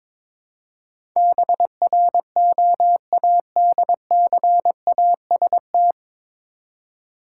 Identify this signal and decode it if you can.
Morse code, heard as a keyed tone, about 22 wpm, 700 Hz. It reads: BROADCAST
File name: 9 Letter Words - First 3 Letters Given - 22wpm